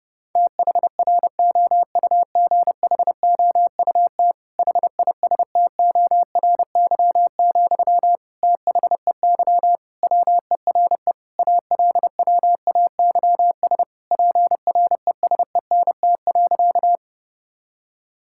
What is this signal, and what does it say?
Morse code, heard as a keyed tone, about 30 wpm, 700 Hz. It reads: THROUGHOUT HISTORY, THEY WERE ALWAYS PRESENT.
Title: Sentences from Top 1000 Words - Review Part 5 - 30wpm